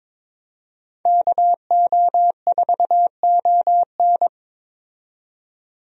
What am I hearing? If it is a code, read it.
Morse code, heard as a keyed tone, about 21 wpm, 700 Hz. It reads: KO4ON
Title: USA Call Signs - 5 Characters - 22wpm